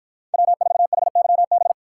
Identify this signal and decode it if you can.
Morse code, heard as a keyed tone, about 53 wpm, 700 Hz. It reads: W4HYB